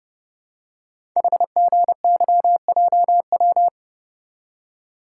Morse code, heard as a keyed tone, about 29 wpm, 700 Hz. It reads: HGYJW
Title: Code Groups - 5 Letters - 30wpm